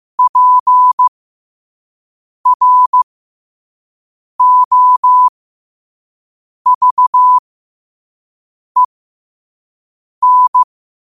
Morse code, heard as a keyed tone, about 15 wpm, spaced as for 5 wpm, 1000 Hz. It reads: PROVEN
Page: New quotes every day in morse code at 5 Words per minute.